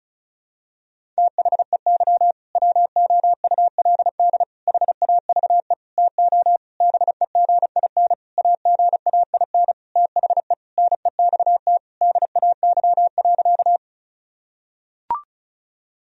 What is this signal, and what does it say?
Morse code, heard as a keyed tone, about 36 wpm, 700 Hz. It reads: THEY WOULD HAVE TO BEGIN AGAIN THE NEXT DAY.
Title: Sentences from Top 1000 Words - Review Part 1 - 35wpm